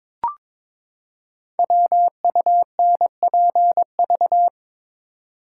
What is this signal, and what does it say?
Morse code, heard as a keyed tone, about 22 wpm, 700 Hz. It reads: WUNPV
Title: Code Groups - 5 Letters - 22wpm